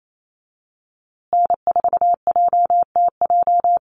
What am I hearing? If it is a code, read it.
Morse code, heard as a keyed tone, about 28 wpm, 700 Hz. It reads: N4JTJ
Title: USA Call Signs Repeated 3x 28wpm